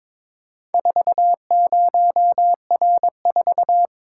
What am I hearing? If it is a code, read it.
Morse code, heard as a keyed tone, about 22 wpm, 690 Hz. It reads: 40R4
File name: practice / 22wpm / Four Letter-Number Combinations - 22wpm